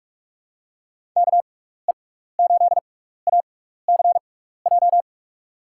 Morse code, heard as a keyed tone, about 44 wpm, spaced as for 15 wpm, 700 Hz. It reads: KE8ACJ
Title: USA Call Signs - 6x Character Spacing - 45wpm